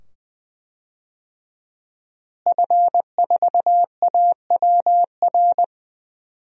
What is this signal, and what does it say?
Morse code, heard as a keyed tone, about 20 wpm, 700 Hz. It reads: F4AWR
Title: DXCC-100 Call Signs - 5 Characters - Repeated 2x - 20wpm